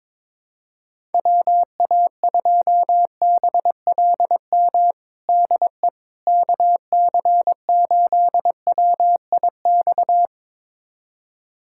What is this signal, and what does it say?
Morse code, heard as a keyed tone, about 22 wpm, 700 Hz. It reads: WA2BLM DE KC8WIX
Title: USA Callsign DE Callsign 22wpm